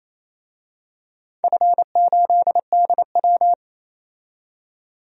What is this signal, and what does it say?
Morse code, heard as a keyed tone, about 28 wpm, 700 Hz. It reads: F8DW
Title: DXCC-100 Call Signs - 4 Characters - 28wpm